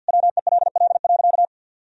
Morse code, heard as a keyed tone, about 51 wpm, 700 Hz. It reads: WELL.
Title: Sentences from Top 1000 Words - Review Part 3 - 50wpm